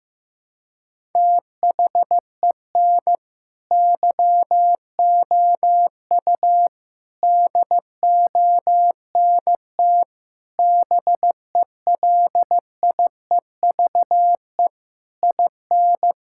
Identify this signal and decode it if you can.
Morse code, heard as a keyed tone, about 16 wpm, 700 Hz. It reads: THEN YOU DONT BELIEVE IN
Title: Sentences from Top 1000 Words - Review Part 6 - 15wpm